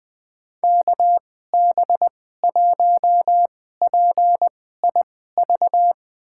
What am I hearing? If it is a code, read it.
Morse code, heard as a keyed tone, about 21 wpm, spaced as for 15 wpm, 700 Hz. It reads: KB1PIV
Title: USA Call Signs - 6 Characters - 2x Character Spacing - 20wpm